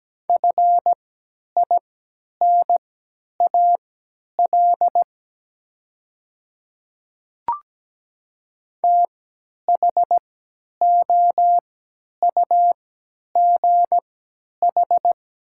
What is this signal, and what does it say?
Morse code, heard as a keyed tone, about 17 wpm, spaced as for 10 wpm, 700 Hz. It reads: FINAL THOUGH
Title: 3x Character Spacing - Top 500 Words - 17wpm